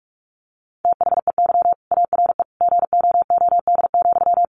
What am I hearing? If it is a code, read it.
Morse code, heard as a keyed tone, about 44 wpm, 700 Hz. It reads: THEY ARE GOOD,
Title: Sentences from Top 1100 Words 45wpm